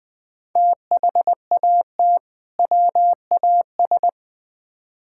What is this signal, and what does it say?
Morse code, heard as a keyed tone, about 21 wpm, 700 Hz. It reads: THAT WAS
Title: Build A Phrase - 2-Word Phrases - 20wpm